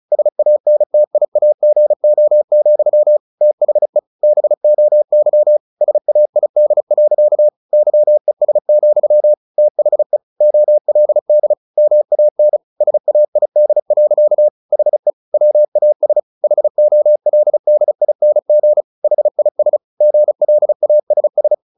Morse code, heard as a keyed tone, about 34 wpm, 600 Hz. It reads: SANTIAGO, THE BOY SAID. YES, THE OLD MAN SAID. HE WAS HOLDING HIS GLASS
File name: the old man and the sea / the old man and the sea 1 35wpm